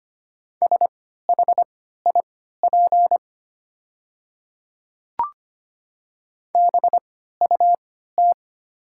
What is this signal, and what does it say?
Morse code, heard as a keyed tone, about 25 wpm, spaced as for 14 wpm, 700 Hz. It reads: SHIP BUT
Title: 3x Character Spacing - Top 500 Words - 25wpm